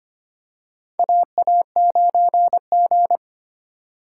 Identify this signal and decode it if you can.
Morse code, heard as a keyed tone, about 25 wpm, 700 Hz. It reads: AA9G
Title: USA Call Signs - 4 Characters - 2x Character Spacing - 25wpm